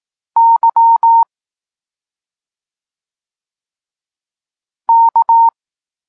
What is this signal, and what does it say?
Morse code, heard as a keyed tone, about 18 wpm, 920 Hz. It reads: Y K
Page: Tekst 32 – Doskonalenie odbioru poznanych liter